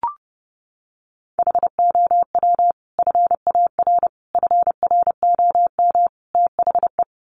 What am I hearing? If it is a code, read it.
Morse code, heard as a keyed tone, about 30 wpm, 700 Hz. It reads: HOW FAR FROM THE
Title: Sentences from Top 400 Words 30wpm